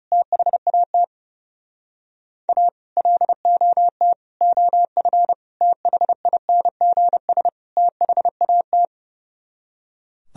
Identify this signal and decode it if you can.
Morse code, heard as a keyed tone, about 35 wpm, 700 Hz. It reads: THAT A LOT OF THINGS THAT
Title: Speed-Racing - Sets of 5 Words 30wpm